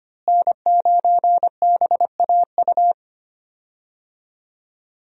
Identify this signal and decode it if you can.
Morse code, heard as a keyed tone, about 25 wpm, 700 Hz. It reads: N9BAU
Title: USA Call Signs - 5 Characters - Repeated 3x - 25wpm